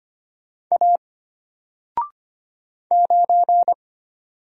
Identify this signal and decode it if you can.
Morse code, heard as a keyed tone, about 25 wpm, 700 Hz. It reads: A 9
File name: Single-Letter-Number-25wpm